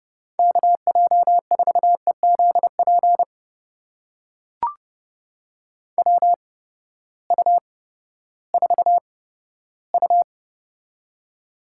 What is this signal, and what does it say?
Morse code, heard as a keyed tone, about 31 wpm, spaced as for 15 wpm, 700 Hz. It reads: KJ4EZP WU4U
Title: USA Call Signs - 8x Character Spacing - 30wpm